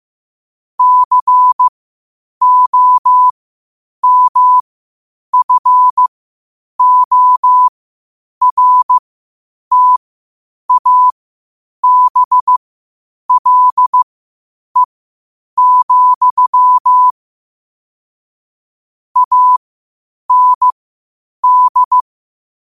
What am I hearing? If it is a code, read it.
Morse code, heard as a keyed tone, about 15 wpm, spaced as for 8 wpm, 1000 Hz. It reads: COMFORTABLE, AND
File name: qotdfiles / 8wpm